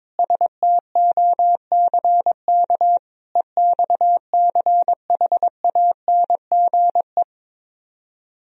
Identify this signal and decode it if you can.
Morse code, heard as a keyed tone, about 22 wpm, 700 Hz. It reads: STOCK EXCHANGE
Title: Speed-Racing - Sets of 5 Words 22wpm